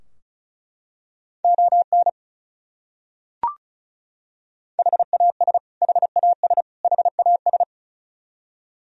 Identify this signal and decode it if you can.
Morse code, heard as a keyed tone, about 36 wpm, 700 Hz. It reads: ON HAS HAS HAS
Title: Top 100 Words Repeated 35wpm